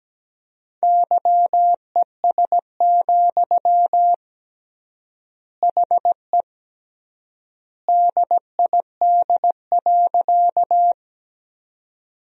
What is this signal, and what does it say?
Morse code, heard as a keyed tone, about 17 wpm, 700 Hz. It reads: YES, HE DID.
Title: 3x Word-Spacing - Sentences from Top 500 Words - 17wpm